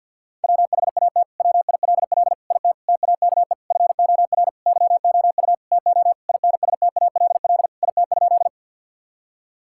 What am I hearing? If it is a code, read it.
Morse code, heard as a keyed tone, about 50 wpm, 700 Hz. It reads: WHAT WILL IT TAKE FOR YOU TO INSTALL IT?
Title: Sentences from Top 1600 Words 50wpm